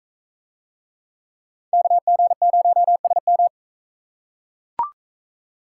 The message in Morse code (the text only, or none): KG0SM